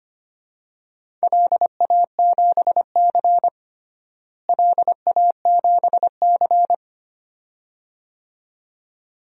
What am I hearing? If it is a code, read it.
Morse code, heard as a keyed tone, about 25 wpm, 700 Hz. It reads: LA7C LA7C
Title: DXCC-100 Call Signs - 4 Characters - Repeated 2x - 25wpm